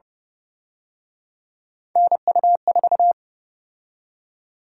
Morse code, heard as a keyed tone, about 31 wpm, 700 Hz. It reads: NU4